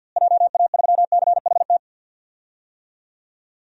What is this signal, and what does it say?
Morse code, heard as a keyed tone, about 46 wpm, 700 Hz. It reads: JA3CHT
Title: DXCC-100 Call Signs - 6 Characters - Repeated 3x - 50wpm